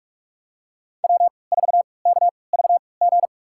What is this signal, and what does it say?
Morse code, heard as a keyed tone, about 47 wpm, spaced as for 26 wpm, 700 Hz. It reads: W4KVG
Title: USA Call Signs - 5 Characters - 3x Character Spacing - 45wpm